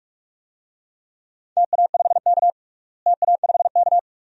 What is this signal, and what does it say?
Morse code, heard as a keyed tone, about 42 wpm, 700 Hz. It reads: TA5K TA5K
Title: DXCC-100 Call Signs - 4 Characters - Repeated 2x - 45wpm